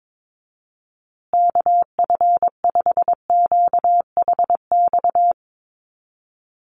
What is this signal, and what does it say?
Morse code, heard as a keyed tone, about 22 wpm, 700 Hz. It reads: KF5QHX
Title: USA Call Signs Repeated 2x 22wpm